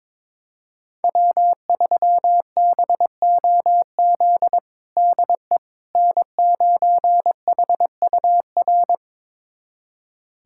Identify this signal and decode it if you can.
Morse code, heard as a keyed tone, about 22 wpm, 700 Hz. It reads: W3BOZ DE N9HUR
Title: USA Callsign DE Callsign - 5 Characters - 22wpm